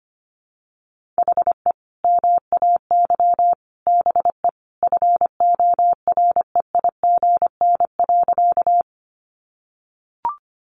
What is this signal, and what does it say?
Morse code, heard as a keyed tone, about 25 wpm, 700 Hz. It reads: HE MAY BE FOREIGN.
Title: Sentences from Top 1400 Words 25wpm